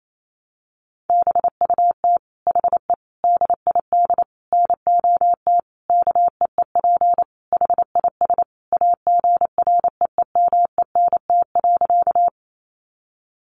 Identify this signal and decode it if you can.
Morse code, heard as a keyed tone, about 28 wpm, 700 Hz. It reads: BUT HE DID NOT KEEP HIS AGREEMENT.
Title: Sentences from Top 1500 Words 28wpm